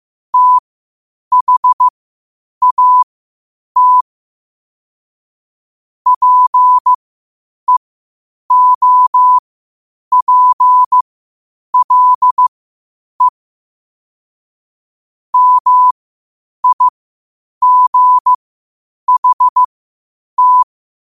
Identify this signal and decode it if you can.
Morse code, heard as a keyed tone, about 15 wpm, spaced as for 8 wpm, 1000 Hz. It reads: THAT PEOPLE MIGHT